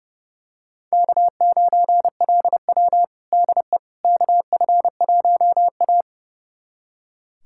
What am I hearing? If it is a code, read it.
Morse code, heard as a keyed tone, about 30 wpm, 700 Hz. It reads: K9LW DE KF1A
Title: USA Callsign DE Callsign - 4 Characters - 30wpm